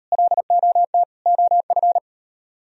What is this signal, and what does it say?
Morse code, heard as a keyed tone, about 38 wpm, 700 Hz. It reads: LOT OF